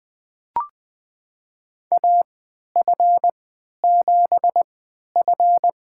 Morse code, heard as a keyed tone, about 20 wpm, spaced as for 11 wpm, 700 Hz. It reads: AF7F
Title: USA Call Signs - 3x Character Spacing - 20wpm